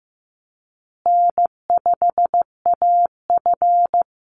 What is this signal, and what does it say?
Morse code, heard as a keyed tone, about 15 wpm, 700 Hz. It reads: N5AF
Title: Easy Call Signs Repeated 3x - 15wpm